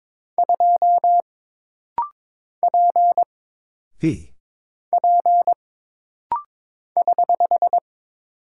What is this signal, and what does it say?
Morse code, heard as a keyed tone, about 22 wpm, 700 Hz. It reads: 2 P P <HH>
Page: This rapid-fire timing is geared to push your ICR (Instant Character Recognition) skills.